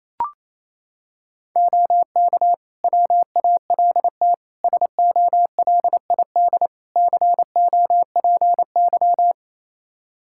Morse code, heard as a keyed tone, about 27 wpm, 700 Hz. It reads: OK WALT SOLID COPY
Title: Practice QSO - Exchange 3 - More Info - 28wpm